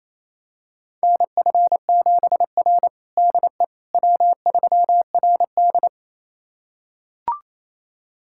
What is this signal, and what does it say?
Morse code, heard as a keyed tone, about 28 wpm, 700 Hz. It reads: NF7R DE W3RD
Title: USA Callsign DE Callsign - 4 Characters - 28wpm